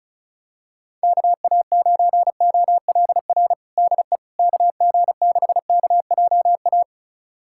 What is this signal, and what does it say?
Morse code, heard as a keyed tone, about 36 wpm, 700 Hz. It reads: KA9OLR DE KG6KJA
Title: USA Callsign DE Callsign - 6 Characters - 35wpm